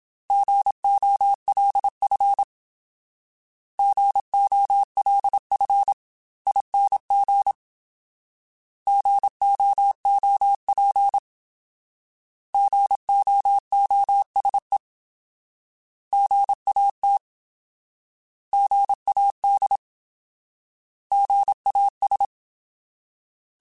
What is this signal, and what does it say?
Morse code, heard as a keyed tone, about 27 wpm, 800 Hz. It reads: GOLF GOLFING GOOP GOOSE GAT GAD GAS
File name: Alphabetical / Alphabetical mp3s / Gs @25